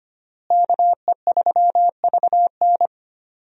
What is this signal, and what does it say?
Morse code, heard as a keyed tone, about 25 wpm, 700 Hz. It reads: KE3VN
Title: USA Call Signs - 5 Characters - Repeated 3x - 25wpm